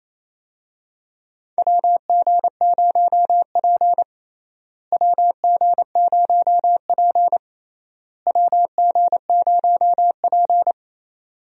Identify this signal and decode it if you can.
Morse code, heard as a keyed tone, about 28 wpm, 700 Hz. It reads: WG0P WG0P WG0P
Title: USA Call Signs - 4 Characters - Repeated 3x - 28wpm